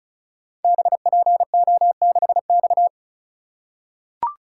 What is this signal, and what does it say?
Morse code, heard as a keyed tone, about 36 wpm, 700 Hz. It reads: BPO6X
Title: Code Groups - 5 Letter:Numbers - 35wpm